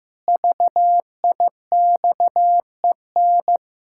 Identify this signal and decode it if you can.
Morse code, heard as a keyed tone, about 15 wpm, 700 Hz. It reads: VIXEN
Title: Pangram Challenge v2 - 15wpm